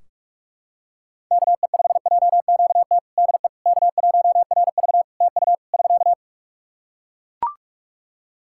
Morse code, heard as a keyed tone, about 45 wpm, 700 Hz. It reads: KE5JQT DE K1RV TU <SK>
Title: Practice QSO - Exchange 4 - Ending - 45wpm